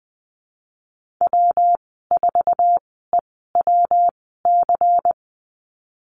Morse code, heard as a keyed tone, about 20 wpm, spaced as for 15 wpm, 700 Hz. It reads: W4EWC
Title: 173 Lesson 28g - Number 9 - Call Signs 20wpm